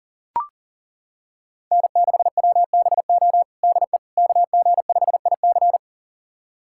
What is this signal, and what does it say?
Morse code, heard as a keyed tone, about 40 wpm, 690 Hz. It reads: N6WBO DE KG5IC
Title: USA Callsign DE Callsign - 5 Characters - 40wpm